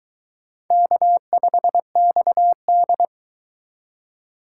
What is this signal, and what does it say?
Morse code, heard as a keyed tone, about 22 wpm, 700 Hz. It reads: K5XD